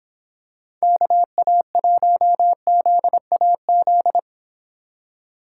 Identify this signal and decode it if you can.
Morse code, heard as a keyed tone, about 26 wpm, 700 Hz. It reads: KA1ZAZ